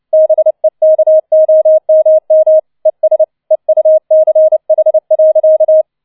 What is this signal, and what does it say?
Morse code, heard as a keyed tone, about 29 wpm, 630 Hz. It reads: BEKOMM ES EUCH.